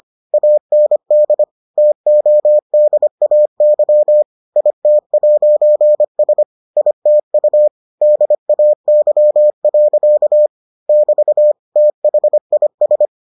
Morse code, heard as a keyed tone, about 25 wpm, 600 Hz. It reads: AND TODAY IT'S ITU DAY. <BT> THIS